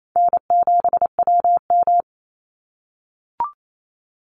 Morse code, heard as a keyed tone, about 28 wpm, 700 Hz. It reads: N7WM
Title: NA SOTA Chaser Callsigns 28wpm